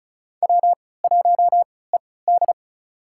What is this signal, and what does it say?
Morse code, heard as a keyed tone, about 36 wpm, spaced as for 20 wpm, 700 Hz. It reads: W1ED
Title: USA Call Signs - 4 Characters - 3x Character Spacing - 35wpm